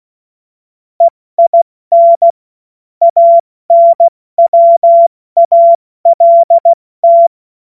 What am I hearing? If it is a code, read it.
Morse code, heard as a keyed tone, about 16 wpm, 680 Hz. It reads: EIN ANWALT
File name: Morsekurs / Witze / Witz 25. Oktober 2022 70 F80